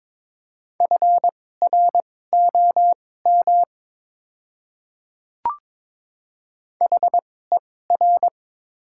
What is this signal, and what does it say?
Morse code, heard as a keyed tone, about 22 wpm, spaced as for 16 wpm, 700 Hz. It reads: FROM HER
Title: 2x Character Spacing - Top 100 Words - 22wpm